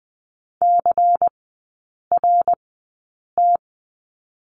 Morse code, heard as a keyed tone, about 19 wpm, spaced as for 8 wpm, 700 Hz. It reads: CRT
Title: 121 Lesson 20e - Number 3 - Three Characters 20wpm